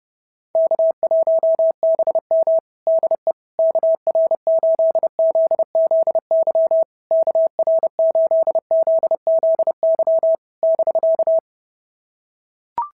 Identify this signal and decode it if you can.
Morse code, heard as a keyed tone, about 28 wpm, 650 Hz. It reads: K1BM DE KR8ZZY KR8ZZY <BK>
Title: SOTA QSO Practice 30wpm